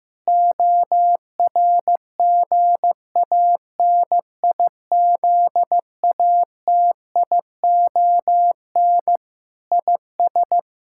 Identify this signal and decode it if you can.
Morse code, heard as a keyed tone, about 16 wpm, 700 Hz. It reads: ORGANIZATION IS
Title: Sentences from Top 1000 Words - Review Part 5 - 15wpm